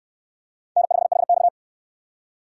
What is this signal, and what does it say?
Morse code, heard as a keyed tone, about 68 wpm, 700 Hz. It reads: N5HX